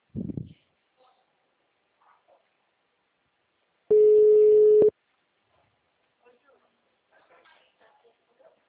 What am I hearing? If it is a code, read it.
Ringback tone (Europe, 425 Hz), with speech.